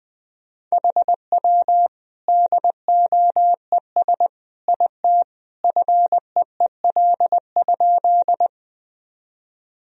Morse code, heard as a keyed tone, about 21 wpm, 700 Hz. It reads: HW DOES IT FEEL?
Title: Sentences from Top 400 Words and Abbreviations 20wpm